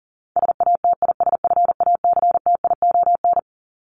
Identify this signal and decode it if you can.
Morse code, heard as a keyed tone, about 40 wpm, 700 Hz. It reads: SATISFACTION
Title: Top 1751-2000 Words 40wpm